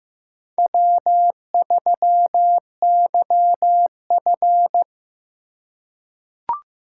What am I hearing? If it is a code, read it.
Morse code, heard as a keyed tone, about 15 wpm, 700 Hz. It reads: W3YF
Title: USA Call Signs - 4x Character Spacing - 15wpm